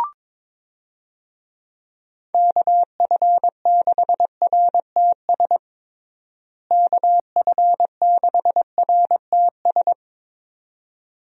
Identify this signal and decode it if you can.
Morse code, heard as a keyed tone, about 22 wpm, 700 Hz. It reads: KF6RTS KF6RTS
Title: USA Call Signs - 6 Characters - Repeated 2x - 22wpm